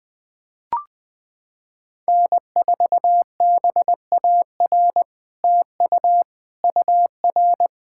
Morse code, heard as a keyed tone, about 21 wpm, 700 Hz. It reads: N4BAR TU UR
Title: SOTA QSO Practice 20wpm